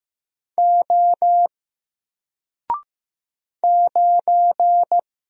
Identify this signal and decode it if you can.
Morse code, heard as a keyed tone, about 15 wpm, 700 Hz. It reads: O 9
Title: Single Letter-Number 15wpm